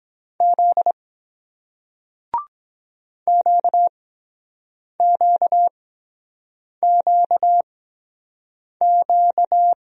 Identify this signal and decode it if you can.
Morse code, heard as a keyed tone, about 24 wpm, spaced as for 7 wpm, 700 Hz. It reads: Z QQQQ